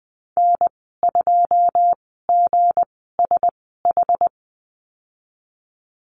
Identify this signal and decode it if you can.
Morse code, heard as a keyed tone, about 20 wpm, spaced as for 15 wpm, 700 Hz. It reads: N2GSH
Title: 167 Lesson 27g - Number 7 - Call Signs 20wpm